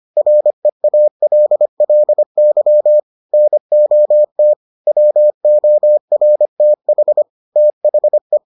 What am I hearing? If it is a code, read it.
Morse code, heard as a keyed tone, about 25 wpm, 600 Hz. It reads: REALLY NOT WORTH THE